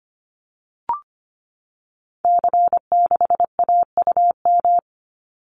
Morse code, heard as a keyed tone, about 25 wpm, 700 Hz. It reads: C6AUM
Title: Call Signs Repeated 3x - 25wpm